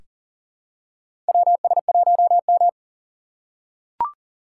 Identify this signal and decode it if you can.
Morse code, heard as a keyed tone, about 40 wpm, 700 Hz. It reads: WS1M
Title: USA Call Signs - 4 Characters - 6x Character Spacing - 40wpm